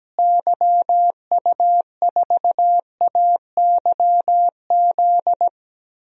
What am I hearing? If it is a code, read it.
Morse code, heard as a keyed tone, about 17 wpm, 700 Hz. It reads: YU4AYZ
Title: DXCC-100 Call Signs - 6 Characters - Repeated 3x - 17wpm